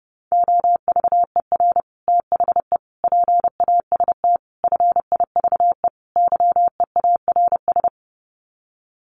Morse code, heard as a keyed tone, about 30 wpm, 700 Hz. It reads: OVER THE PAST FIVE YEARS
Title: Sets of 5 Words Encore 30wpm